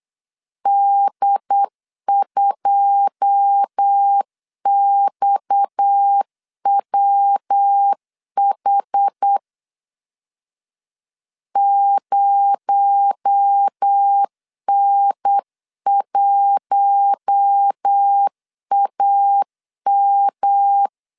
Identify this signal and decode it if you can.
Morse code, heard as a keyed tone, about 8 wpm, 800 Hz. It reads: D2XWH 0N1AM
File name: CW / DET / 1 9 1500C S C 8wpm 5653